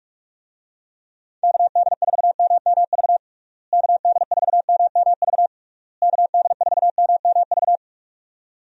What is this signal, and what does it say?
Morse code, heard as a keyed tone, about 42 wpm, 700 Hz. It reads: KD4MMV KD4MMV KD4MMV
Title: USA Call Signs - 6 Characters - Repeated 3x - 45wpm